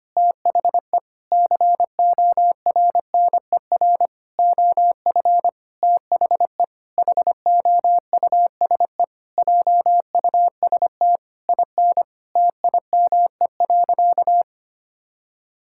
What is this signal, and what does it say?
Morse code, heard as a keyed tone, about 25 wpm, 700 Hz. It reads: THE CORNER OF THE HOUSE JUST IN TIME.
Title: Sentences from Top 1600 Words 25wpm